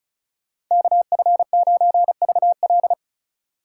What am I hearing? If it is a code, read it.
Morse code, heard as a keyed tone, about 35 wpm, 700 Hz. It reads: KF9VL
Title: Speed Racing - USA 2x2 Call Signs 30wpm